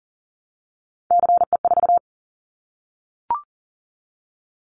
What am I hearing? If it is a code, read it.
Morse code, heard as a keyed tone, about 40 wpm, 700 Hz. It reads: CE4
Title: Callsign Prefixes for Top 100 DXCC - 40wpm